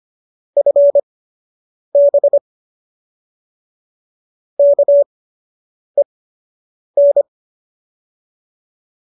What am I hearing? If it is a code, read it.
Morse code, heard as a keyed tone, about 25 wpm, spaced as for 8 wpm, 590 Hz. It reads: FB KEN
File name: qso003 08